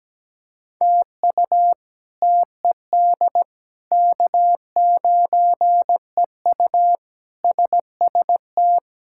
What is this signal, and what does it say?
Morse code, heard as a keyed tone, about 17 wpm, 700 Hz. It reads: TU TED K9EU SST
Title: K1USN SST QSO Practice - 17wpm